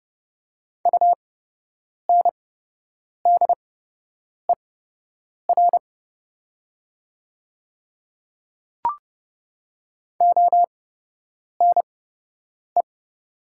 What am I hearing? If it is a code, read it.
Morse code, heard as a keyed tone, about 29 wpm, spaced as for 8 wpm, 700 Hz. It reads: UNDER ONE
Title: 8x Character Spacing - Top 300 Words - 30wpm